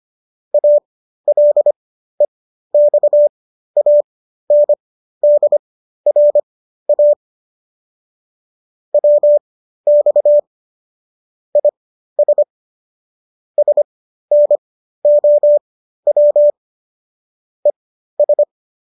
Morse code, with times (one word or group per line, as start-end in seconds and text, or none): 0.5-7.1 ALEXANDRA
8.9-10.4 WX
11.6-12.4 IS
13.6-16.5 SNOW
17.7-18.4 ES